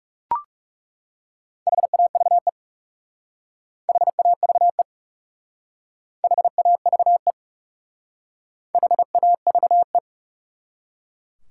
Morse code, heard as a keyed tone, about 38 wpm, 700 Hz. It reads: HAVE HAVE HAVE HAVE